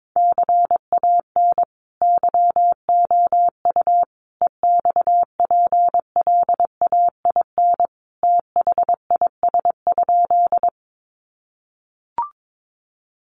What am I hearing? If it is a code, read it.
Morse code, heard as a keyed tone, about 22 wpm, 700 Hz. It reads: CAN YOU EXPLAIN THIS?
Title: Sentences from Top 1100 Words 22wpm